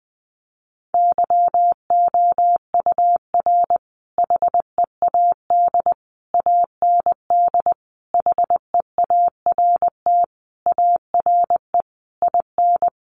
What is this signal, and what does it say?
Morse code, heard as a keyed tone, about 20 wpm, 700 Hz. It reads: YOUR HEAD AND HEART ARE IN
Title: Sentences from Top 1300 Words 20wpm